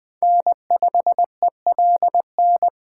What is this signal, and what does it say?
Morse code, heard as a keyed tone, about 20 wpm, 700 Hz. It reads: N5ELN